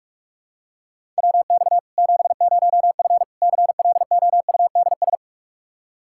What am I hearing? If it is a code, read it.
Morse code, heard as a keyed tone, about 46 wpm, 700 Hz. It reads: WX 70F CLOUDS
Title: K1USN SST QSO Practice 45wpm